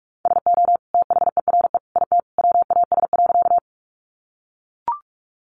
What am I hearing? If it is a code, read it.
Morse code, heard as a keyed tone, about 44 wpm, 700 Hz. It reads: SO THERE IT WAS.
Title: Sentences from Top 100 Words 45wpm